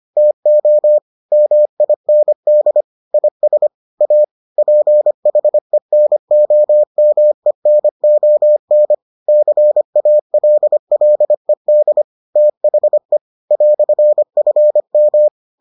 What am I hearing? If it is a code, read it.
Morse code, heard as a keyed tone, about 25 wpm, 600 Hz. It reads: TO MIND IS A PHENOMENON CALLED THE "FM